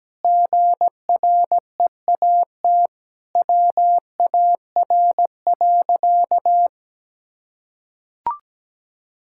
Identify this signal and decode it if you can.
Morse code, heard as a keyed tone, about 17 wpm, 700 Hz. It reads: GREAT WAR.
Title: Sentences from Top 1600 Words 17wpm